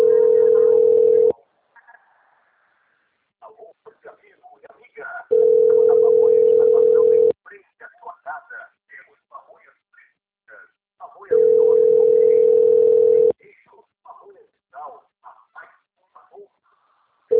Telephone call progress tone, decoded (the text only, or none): ringback tone